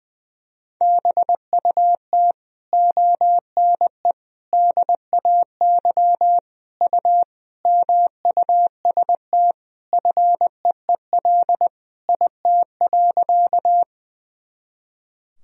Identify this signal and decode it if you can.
Morse code, heard as a keyed tone, about 20 wpm, 700 Hz. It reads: BUT ONE DAY U MUST FEEL IT.
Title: Sentences from Top 400 Words and Abbreviations 20wpm